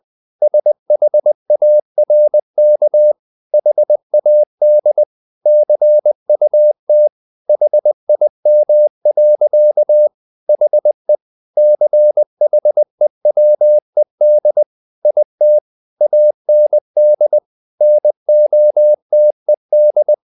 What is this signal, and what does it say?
Morse code, heard as a keyed tone, about 20 wpm, 600 Hz. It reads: SHARK HAD CUT HIM. HE CHEWED IT AND NOTED
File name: the old man and the sea part 2 20wpm